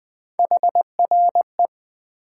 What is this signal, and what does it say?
Morse code, heard as a keyed tone, about 20 wpm, 700 Hz. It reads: HRE
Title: Common QSO Phrases - 4x Word-Spacing - 20wpm